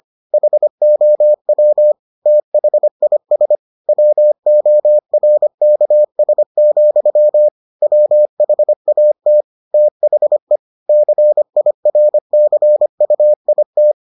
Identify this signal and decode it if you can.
Morse code, heard as a keyed tone, about 25 wpm, 600 Hz. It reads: HOW THIS WORKS, WHAT THE CIRCUIT